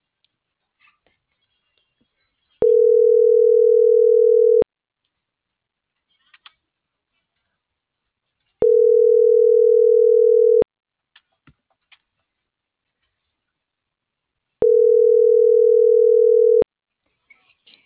Ringback tone (North America), 440 + 480 Hz.